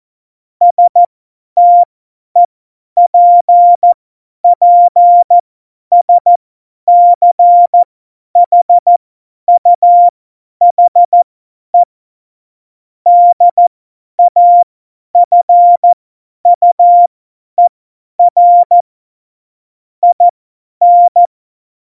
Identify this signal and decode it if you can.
Morse code, heard as a keyed tone, about 14 wpm, spaced as for 10 wpm, 700 Hz. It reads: STEPPSCHUHE DAFUER IN
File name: Morsekurs / Witze / Witz 07 Juli 2020, 50 F70